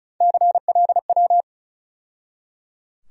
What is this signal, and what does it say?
Morse code, heard as a keyed tone, about 34 wpm, 700 Hz. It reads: CLW
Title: USA Call Sign Postfixes - 3 Letters 35wpm